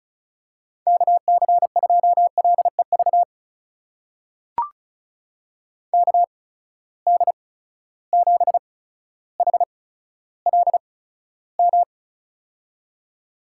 Morse code, heard as a keyed tone, about 34 wpm, spaced as for 15 wpm, 700 Hz. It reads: KC2LEV KD7HLM
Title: USA Call Signs - 6 Characters - 8x Character Spacing - 35wpm